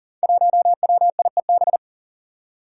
Morse code, heard as a keyed tone, about 40 wpm, 700 Hz. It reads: 1WIEB
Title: Five Challenging Letter-Number Combinations 40wpm